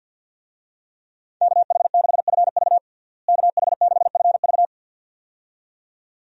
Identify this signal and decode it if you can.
Morse code, heard as a keyed tone, about 52 wpm, 700 Hz. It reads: KH6FV KH6FV
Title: USA Call Signs - 5 Characters - Repeated 2x - 50wpm